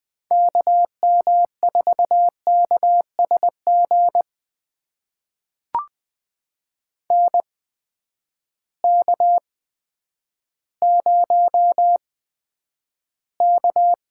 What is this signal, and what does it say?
Morse code, heard as a keyed tone, about 21 wpm, spaced as for 10 wpm, 700 Hz. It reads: KM4KSG NK0K
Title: USA Call Signs - 8x Character Spacing - 20wpm